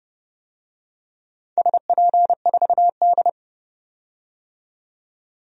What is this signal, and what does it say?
Morse code, heard as a keyed tone, about 30 wpm, 700 Hz. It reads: SP4D
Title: DXCC-100 Call Signs - 4 Characters - 30wpm